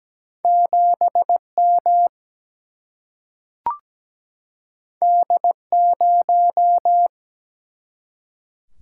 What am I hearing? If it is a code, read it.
Morse code, heard as a keyed tone, about 17 wpm, 700 Hz. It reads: 7 M D 0
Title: Single-Letter Single-Number 17wpm